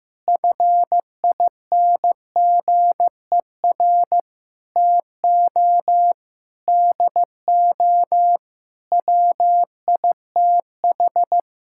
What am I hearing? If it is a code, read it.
Morse code, heard as a keyed tone, about 15 wpm, 700 Hz. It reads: FINGER TO DO WITH
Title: Sentences from Top 1600 Words 15wpm